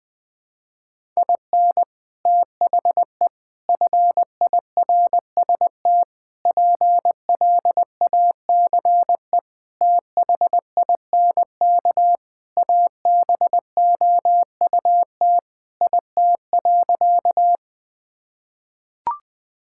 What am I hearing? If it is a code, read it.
Morse code, heard as a keyed tone, about 20 wpm, 700 Hz. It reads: IN THE FIRST PLACE THINK ABOUT IT.
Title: Sentences with 3-Word Phrases - Spoken Beginning - 20wpm